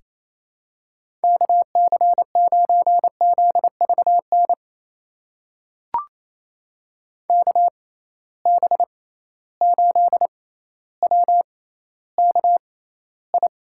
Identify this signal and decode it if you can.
Morse code, heard as a keyed tone, about 28 wpm, spaced as for 14 wpm, 700 Hz. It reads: KC9ZVN KB8WKI
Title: USA Call Signs - 6 Characters - 6x Character Spacing - 28wpm